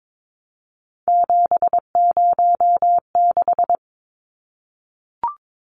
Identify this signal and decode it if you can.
Morse code, heard as a keyed tone, about 22 wpm, 700 Hz. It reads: 706
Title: Numbers - 3-Digit 22wpm